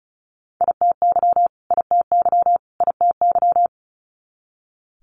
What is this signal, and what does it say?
Morse code, heard as a keyed tone, about 36 wpm, 700 Hz. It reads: ITY ITY ITY
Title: Multi-letter Phonemes Repeated 3x 35wpm